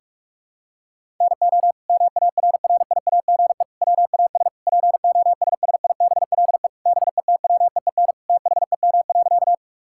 Morse code, heard as a keyed tone, about 46 wpm, 700 Hz. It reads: NO MARRIAGE WAS POSSIBLE BETWEEN THEM.
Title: Sentences from Top 1000 Words - Review Part 5 - 45wpm